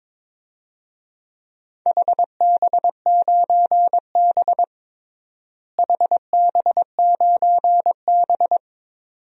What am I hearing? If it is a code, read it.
Morse code, heard as a keyed tone, about 22 wpm, 700 Hz. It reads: HB9B HB9B
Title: DXCC-100 Call Signs - 4 Characters - Repeated 2x - 22wpm